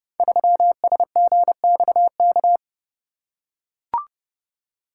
Morse code, heard as a keyed tone, about 28 wpm, 700 Hz. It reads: 3SGXK
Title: Code Groups - 5 Letter:Numbers - 30wpm